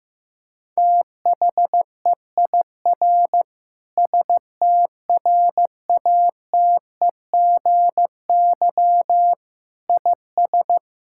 Morse code, heard as a keyed tone, about 15 wpm, 700 Hz. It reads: THEIR STRATEGY IS
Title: Sentences from Top 1000 Words - Review Part 5 - 15wpm